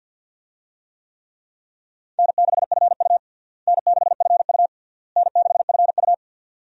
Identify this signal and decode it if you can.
Morse code, heard as a keyed tone, about 47 wpm, 700 Hz. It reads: N6FU N6FU N6FU
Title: USA Call Signs - 4 Characters - Repeated 3x - 50wpm